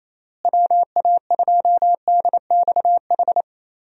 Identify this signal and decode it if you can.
Morse code, heard as a keyed tone, about 28 wpm, 700 Hz. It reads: WA2DXH
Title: USA Call Signs - 6 Characters - Repeated 2x - 28wpm